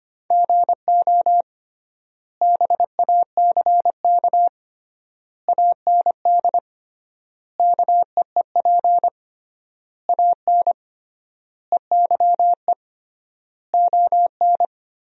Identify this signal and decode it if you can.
Morse code, heard as a keyed tone, about 25 wpm, 700 Hz. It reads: GO BACK AND KEEP AN EYE ON
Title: 3x Word-Spacing - Sentences from Top 300 Words - 25wpm